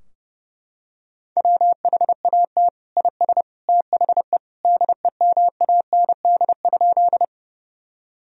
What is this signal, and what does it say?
Morse code, heard as a keyed tone, about 32 wpm, 700 Hz. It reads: WHAT IS THE DEMAND?
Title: Sentences from Top 1000 Words - Review Part 4 - 30wpm